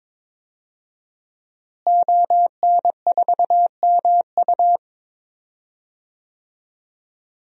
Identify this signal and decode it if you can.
Morse code, heard as a keyed tone, about 22 wpm, 700 Hz. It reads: ON4MU
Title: DXCC-100 Call Signs - 5 Characters - Repeated 3x - 22wpm